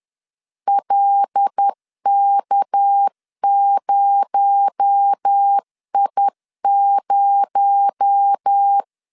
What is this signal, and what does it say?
Morse code, heard as a keyed tone, about 10 wpm, 800 Hz. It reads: LK0I0
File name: LEZIONI / 6 PKX0 / 6 PKX0 10wpm 2707